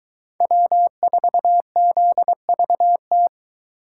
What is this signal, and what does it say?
Morse code, heard as a keyed tone, about 22 wpm, 700 Hz. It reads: W4ZVT